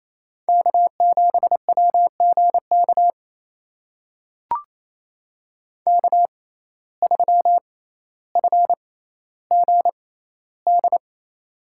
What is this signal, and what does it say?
Morse code, heard as a keyed tone, about 28 wpm, spaced as for 14 wpm, 700 Hz. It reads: K7WGK K3FGD
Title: USA Call Signs - 5 Characters - 6x Character Spacing - 28wpm